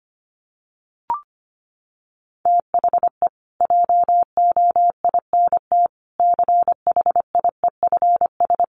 Morse code, heard as a keyed tone, about 25 wpm, 700 Hz. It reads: THE JOINT CHIEFS
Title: Sets of 5 Words Encore2 25wpm